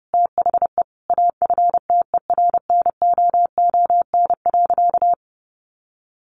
Morse code, heard as a keyed tone, about 30 wpm, 700 Hz. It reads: THE AFTERNOON.
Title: Sentences from Top 1500 Words 30wpm